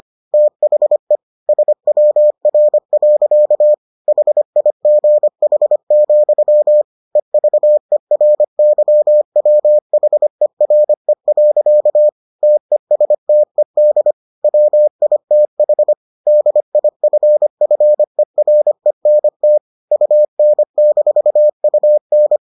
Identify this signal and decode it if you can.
Morse code, heard as a keyed tone, about 25 wpm, 600 Hz. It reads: THE SWR. HIGH, EVERYWHERE. TESTED WITH DIFFERENT UN-UN